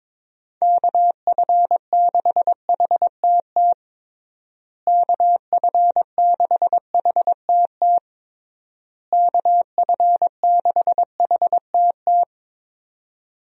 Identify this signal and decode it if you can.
Morse code, heard as a keyed tone, about 22 wpm, 700 Hz. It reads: KF6HTT KF6HTT KF6HTT
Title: USA Call Signs - 6 Characters - Repeated 3x - 22wpm